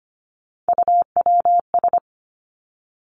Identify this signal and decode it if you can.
Morse code, heard as a keyed tone, about 25 wpm, 700 Hz. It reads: UWS
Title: Three Challenging Letter-Number Combinations 25wpm